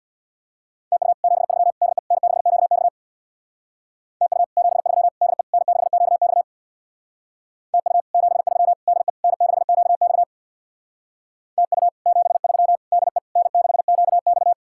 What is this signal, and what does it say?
Morse code, heard as a keyed tone, about 58 wpm, 700 Hz. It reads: TU 73 DE N6QX TU 73 DE N6QX TU 73 DE N6QX TU 73 DE N6QX
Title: Speed-Racing Common QSO Phrases Encore - 50wpm